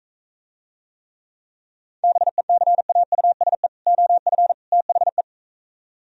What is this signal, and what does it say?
Morse code, heard as a keyed tone, about 45 wpm, 700 Hz. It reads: BECAUSE OF THE